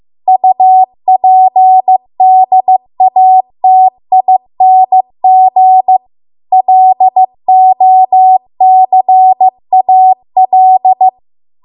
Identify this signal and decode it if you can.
Morse code, heard as a keyed tone, about 15 wpm, 750 Hz. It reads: UPDATING LOCAL